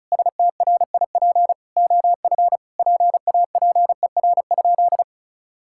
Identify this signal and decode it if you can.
Morse code, heard as a keyed tone, about 34 wpm, 700 Hz. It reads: STRIP OF PAPER?
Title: Sentences from Top 1600 Words 35wpm